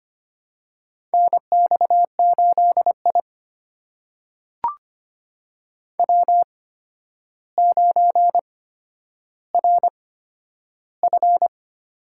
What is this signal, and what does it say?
Morse code, heard as a keyed tone, about 25 wpm, spaced as for 11 wpm, 700 Hz. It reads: NX8I W9RF
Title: USA Call Signs - 4 Characters - 8x Character Spacing - 25wpm